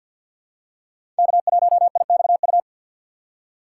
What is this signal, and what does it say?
Morse code, heard as a keyed tone, about 51 wpm, 700 Hz. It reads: K1IXU
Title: USA Call Signs - 5 Characters - 2x Character Spacing - 50wpm